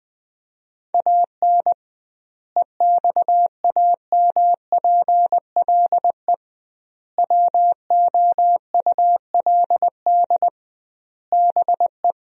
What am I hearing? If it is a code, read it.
Morse code, heard as a keyed tone, about 21 wpm, 700 Hz. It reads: AN EXAMPLE WOULD BE
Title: 2x Word-Spacing - Sentences from Top 400 Words - 20wpm